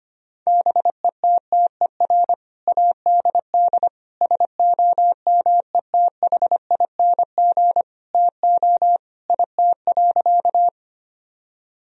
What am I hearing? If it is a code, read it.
Morse code, heard as a keyed tone, about 25 wpm, 700 Hz. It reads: BETTER ADD SOMETHING TO IT.
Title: Sentences from Top 1000 Words - Review Part 1 - 25wpm